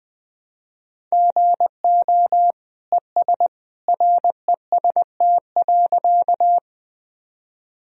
Morse code, heard as a keyed tone, about 21 wpm, 700 Hz. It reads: GO ES REST.
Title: Sentences from Top 500 Words and Abbreviations 20wpm